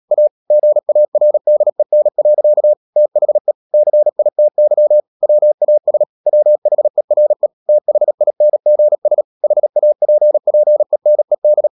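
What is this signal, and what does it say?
Morse code, heard as a keyed tone, about 37 wpm, 600 Hz. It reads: A GARDEN. THE CITY WAS WHERE THINGS HAPPENED
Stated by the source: Morse code practice at 35–39 wpm sending English text of Ken Follett's novel Never, from CW med Gnister.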